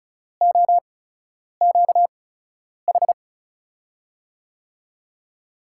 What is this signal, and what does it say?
Morse code, heard as a keyed tone, about 36 wpm, spaced as for 10 wpm, 700 Hz. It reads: OQH
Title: USA Call Signs - 6 Characters - 8x Character Spacing - 35wpm